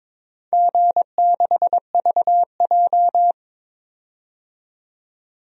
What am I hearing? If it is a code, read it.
Morse code, heard as a keyed tone, about 21 wpm, 700 Hz. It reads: G6VJ
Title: DXCC-100 Call Signs - 4 Characters - 22wpm